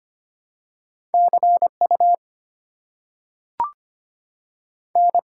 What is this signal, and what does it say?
Morse code, heard as a keyed tone, about 25 wpm, 690 Hz. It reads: CU N
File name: Top 90 QSO Elements - 6x Character Spacing - 25wpm